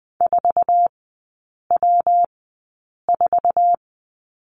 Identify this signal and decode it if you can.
Morse code, heard as a keyed tone, about 20 wpm, spaced as for 8 wpm, 700 Hz. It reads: 4W4
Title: 121 Lesson 20e - Number 3 - Three Characters 20wpm